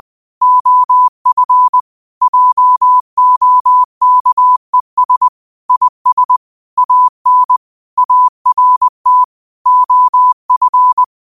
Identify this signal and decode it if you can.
Morse code, heard as a keyed tone, about 20 wpm, 1000 Hz. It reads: OF JOKES IS AN ART OF